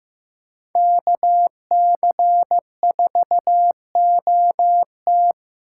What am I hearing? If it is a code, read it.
Morse code, heard as a keyed tone, about 18 wpm, 700 Hz. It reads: KC4OT
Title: Speed Racing - USA 2x2 Call Signs 15wpm